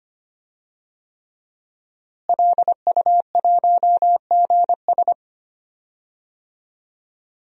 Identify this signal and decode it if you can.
Morse code, heard as a keyed tone, about 25 wpm, 700 Hz. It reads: LU1GS
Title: DXCC-100 Call Signs - 5 Characters - Repeated 3x - 25wpm